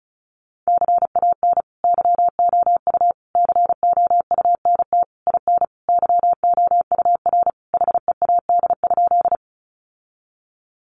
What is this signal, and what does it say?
Morse code, heard as a keyed tone, about 34 wpm, 700 Hz. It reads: CAN YOU COUNT IN YOUR HEAD?
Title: Sentences from Top 1500 Words 35wpm